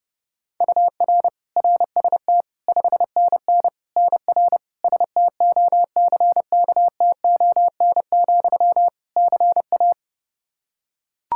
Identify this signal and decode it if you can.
Morse code, heard as a keyed tone, about 30 wpm, 700 Hz. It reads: UR RST 5NN NR STOCKTON, CA
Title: Practice QSO - Exchange 1 - Beginning - 30wpm